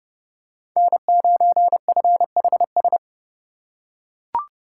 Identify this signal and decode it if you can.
Morse code, heard as a keyed tone, about 30 wpm, 700 Hz. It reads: N9FHS
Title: USA Call Signs - 5 Characters - 4x Character Spacing - 30wpm